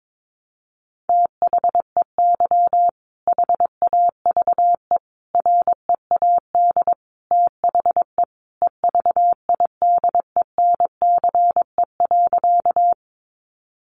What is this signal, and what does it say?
Morse code, heard as a keyed tone, about 22 wpm, 700 Hz. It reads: THEY HAVE READ THE EVIDENCE.
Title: Sentences from Top 1300 Words 22wpm